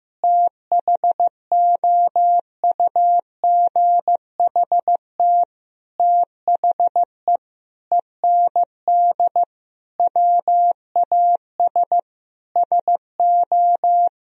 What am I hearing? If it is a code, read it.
Morse code, heard as a keyed tone, about 15 wpm, 700 Hz. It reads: THOUGHT THE END WAS SO
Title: Sentences from Top 1000 Words - Review Part 1 - 15wpm